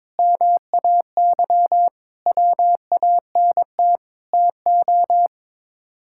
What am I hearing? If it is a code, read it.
Morse code, heard as a keyed tone, about 22 wpm, 700 Hz. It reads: MAY WANT TO
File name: practice / 22wpm / Sets of 8 Words - 22wpm